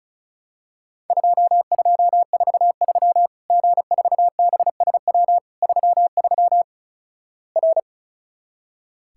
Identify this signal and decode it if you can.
Morse code, heard as a keyed tone, about 34 wpm, 700 Hz. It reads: 2243 G4BSW 33 R
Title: Worked All Europe DX Contest - QTC Exchange - 35wpm